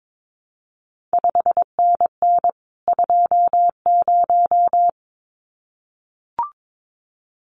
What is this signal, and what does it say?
Morse code, heard as a keyed tone, about 22 wpm, 700 Hz. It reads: 5NN 20
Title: CQ WW DX Contest Exchange - 22wpm